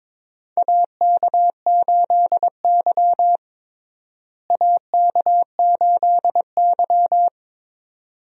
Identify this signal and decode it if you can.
Morse code, heard as a keyed tone, about 22 wpm, 700 Hz. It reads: AK8Y AK8Y
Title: USA Call Signs - 4 Characters - Repeated 2x - 22wpm